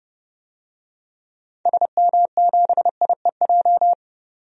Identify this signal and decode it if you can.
Morse code, heard as a keyed tone, about 30 wpm, 700 Hz. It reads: SM7IEJ
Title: DXCC-100 Call Signs - 6 Characters - 30wpm